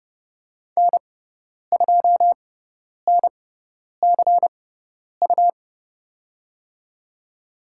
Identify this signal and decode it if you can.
Morse code, heard as a keyed tone, about 31 wpm, spaced as for 10 wpm, 700 Hz. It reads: N2NCU
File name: Lesson 18g - Letter m - Call Signs 30wpm